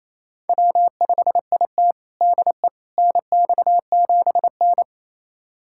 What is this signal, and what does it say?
Morse code, heard as a keyed tone, about 28 wpm, 700 Hz. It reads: W5IT DE NX7N
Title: USA Callsign DE Callsign - 4 Characters - 28wpm